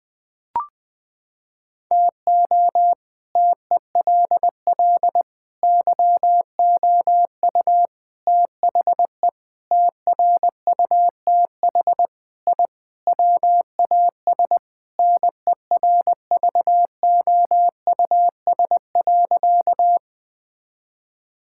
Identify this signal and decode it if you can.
Morse code, heard as a keyed tone, about 20 wpm, 700 Hz. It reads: TO TELL YOU THE TRUTH I WAS NERVOUS.
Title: Sentences with 5-Word Phrases - 20wpm